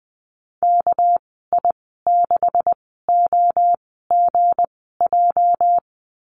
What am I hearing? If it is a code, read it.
Morse code, heard as a keyed tone, about 20 wpm, spaced as for 14 wpm, 700 Hz. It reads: KI6OGJ
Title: 213 Lesson 34g - Number 8 - Call Signs 20wpm